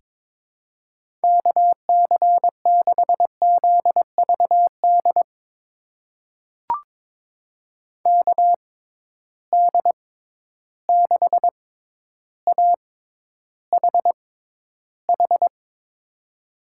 Morse code, heard as a keyed tone, about 22 wpm, spaced as for 11 wpm, 700 Hz. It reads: KC6ZVD KD6AHH
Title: USA Call Signs - 6 Characters - 6x Character Spacing - 22wpm